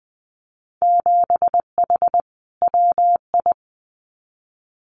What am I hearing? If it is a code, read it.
Morse code, heard as a keyed tone, about 19 wpm, 700 Hz. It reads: 7H WI
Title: Winter Field Day Exchanges Encore 20wpm